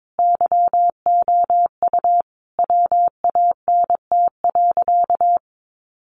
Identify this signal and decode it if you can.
Morse code, heard as a keyed tone, about 22 wpm, 700 Hz. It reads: YOU WANT.
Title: Sentences from Top 1300 Words 22wpm